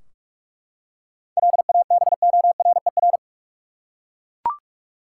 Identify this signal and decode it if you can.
Morse code, heard as a keyed tone, about 42 wpm, 700 Hz. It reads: LABORER
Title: Practice QSO - Exchange 3 - More Info - 45wpm